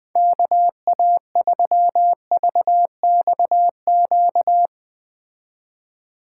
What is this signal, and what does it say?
Morse code, heard as a keyed tone, about 20 wpm, 700 Hz. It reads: KA3VXQ